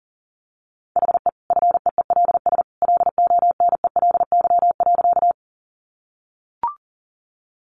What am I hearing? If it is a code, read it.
Morse code, heard as a keyed tone, about 37 wpm, 700 Hz. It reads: HE FEELS LONELY.
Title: Easy Phrases 40wpm